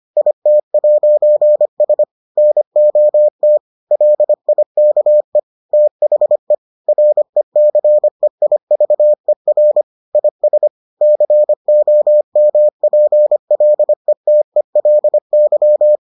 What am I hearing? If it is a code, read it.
Morse code, heard as a keyed tone, about 25 wpm, 600 Hz. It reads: IT'S NOT LIKE THE RECEIVER IS COMPLETELY